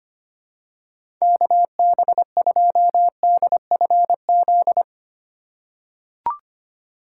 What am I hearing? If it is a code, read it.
Morse code, heard as a keyed tone, about 25 wpm, 700 Hz. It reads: KB2DFZ
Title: USA Call Signs - 6 Characters - 2x Character Spacing - 25wpm